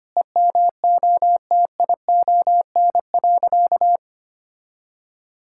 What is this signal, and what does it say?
Morse code, heard as a keyed tone, about 25 wpm, 700 Hz. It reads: EMOTION.
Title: Sentences from Top 1600 Words 25wpm